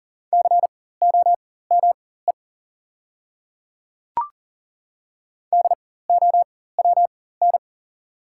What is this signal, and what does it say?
Morse code, heard as a keyed tone, about 40 wpm, spaced as for 19 wpm, 690 Hz. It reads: COME DOWN
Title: 4x Character Spacing - Top 100 Words - 40wpm